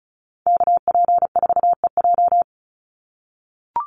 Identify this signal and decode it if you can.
Morse code, heard as a keyed tone, about 34 wpm, 700 Hz. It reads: KP4EJ
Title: Call Signs - 35wpm